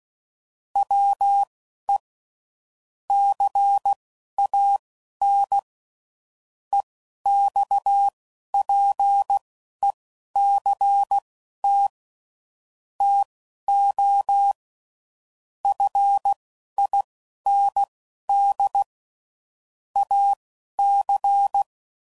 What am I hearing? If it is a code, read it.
Morse code, heard as a keyed tone, about 16 wpm, spaced as for 11 wpm, 800 Hz. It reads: WE CAN EXPECT TO FIND AC
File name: CW Code Groups @10WPM / Technical Subjects / SwPwrSup